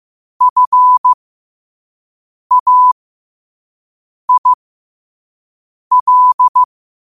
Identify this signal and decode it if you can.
Morse code, heard as a keyed tone, about 15 wpm, spaced as for 5 wpm, 1000 Hz. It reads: FAIL
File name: qotdfiles / 5wpm